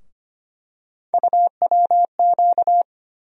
Morse code, heard as a keyed tone, about 25 wpm, 700 Hz. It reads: UWQ
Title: USA Call Sign Postfixes - 3 Letters 25wpm